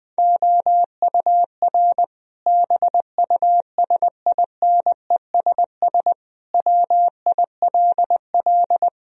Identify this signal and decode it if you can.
Morse code, heard as a keyed tone, about 20 wpm, 700 Hz. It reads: OUR BUSINESS WILL
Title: Sentences from Top 1000 Words - Review Part 4 - 20wpm